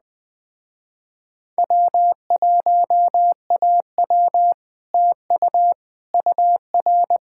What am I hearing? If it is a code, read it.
Morse code, heard as a keyed tone, about 20 wpm, 700 Hz. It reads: W1AW TU UR